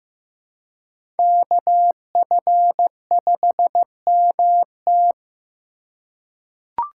Morse code, heard as a keyed tone, about 15 wpm, 700 Hz. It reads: KF5MT
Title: USA Call Signs - 5 Characters - 2x Character Spacing - 15wpm